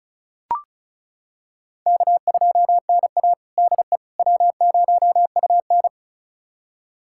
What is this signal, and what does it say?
Morse code, heard as a keyed tone, about 36 wpm, 690 Hz. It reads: K2NA DE W0UN
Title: USA Callsign DE Callsign - 4 Characters - 35wpm